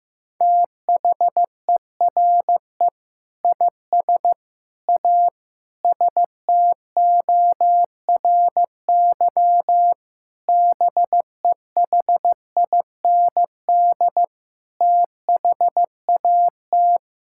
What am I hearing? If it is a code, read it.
Morse code, heard as a keyed tone, about 15 wpm, 700 Hz. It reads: THERE IS A STORY BEHIND THAT
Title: Sentences from Top 1000 Words - Review Part 5 - 15wpm